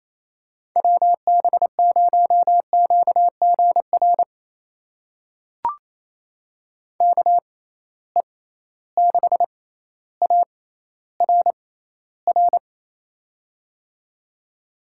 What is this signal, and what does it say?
Morse code, heard as a keyed tone, about 28 wpm, spaced as for 14 wpm, 700 Hz. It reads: WB0QGR KE6ARR
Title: USA Call Signs - 6 Characters - 6x Character Spacing - 28wpm